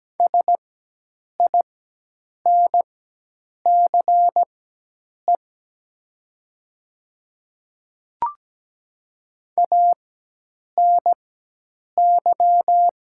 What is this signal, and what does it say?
Morse code, heard as a keyed tone, about 17 wpm, spaced as for 8 wpm, 700 Hz. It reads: SINCE ANY
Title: 4x Character Spacing - Top 300 Words - 17wpm